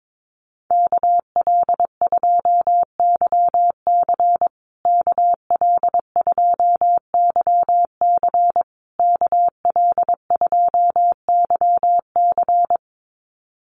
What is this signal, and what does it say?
Morse code, heard as a keyed tone, about 22 wpm, 700 Hz. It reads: KL2YC KL2YC KL2YC
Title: USA Call Signs Repeated 3x 22wpm